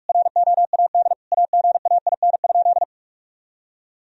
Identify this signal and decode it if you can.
Morse code, heard as a keyed tone, about 46 wpm, 700 Hz. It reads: ROAD AGAIN?
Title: Sentences from Top 1000 Words - Review Part 2 - 45wpm